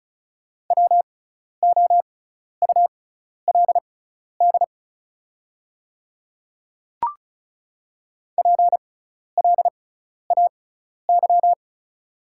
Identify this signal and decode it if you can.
Morse code, heard as a keyed tone, about 36 wpm, spaced as for 12 wpm, 690 Hz. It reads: WOULD PLAY
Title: 6x Character Spacing - Top 500 Words - 35wpm